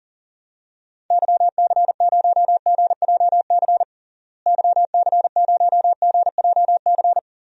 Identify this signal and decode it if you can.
Morse code, heard as a keyed tone, about 40 wpm, 700 Hz. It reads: YC0GJC YC0GJC
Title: DXCC-100 Call Signs - 6 Characters - Repeated 3x - 40wpm